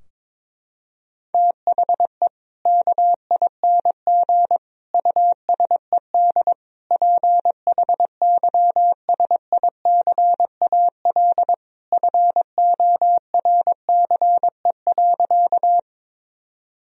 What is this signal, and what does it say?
Morse code, heard as a keyed tone, about 22 wpm, 700 Hz. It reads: THE KING USED PHYSICAL FORCE.
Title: Sentences from Top 1000 Words - Review Part 4 - 22wpm